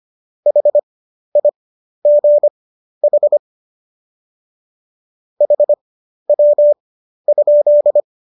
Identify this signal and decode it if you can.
Morse code, heard as a keyed tone, about 25 wpm, spaced as for 12 wpm, 590 Hz. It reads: HIGH HW?
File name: qso012 12